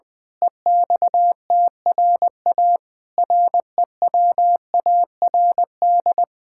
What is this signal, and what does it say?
Morse code, heard as a keyed tone, about 21 wpm, 700 Hz. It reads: EXTRA REWARD
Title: Pangram Challenge v2 - 20wpm